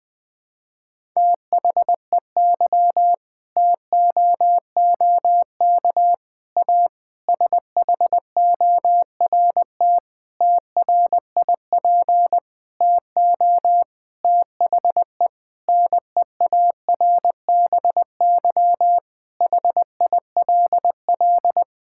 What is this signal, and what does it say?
Morse code, heard as a keyed tone, about 20 wpm, 700 Hz. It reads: THEY TOOK A SHORT TRIP TO THE NEARBY HILL
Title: Difficult Sentences with Hints - 20wpm